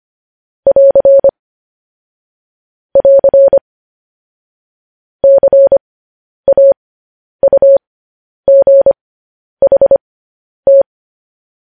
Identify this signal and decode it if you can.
Morse code, heard as a keyed tone, about 25 wpm, spaced as for 10 wpm, 550 Hz. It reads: <AR> <AR> CAUGHT
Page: news-headlines-in-morse-code-at-10-wpm